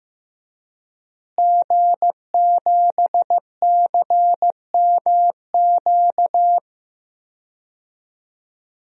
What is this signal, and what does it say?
Morse code, heard as a keyed tone, about 15 wpm, 700 Hz. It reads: G7CMQ
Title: DXCC-100 Call Signs - 5 Characters - Repeated 2x - 15wpm